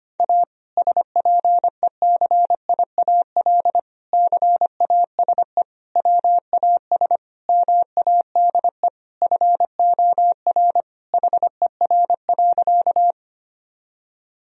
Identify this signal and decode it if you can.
Morse code, heard as a keyed tone, about 25 wpm, 700 Hz. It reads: A SPECIAL CASE WAS MADE FOR HER.
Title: Sentences from Top 1000 Words - Review Part 3 - 25wpm